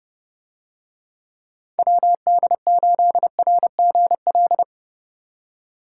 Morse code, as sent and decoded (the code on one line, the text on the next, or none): .-- -.. ---.. .-. --. .-..
WD8RGL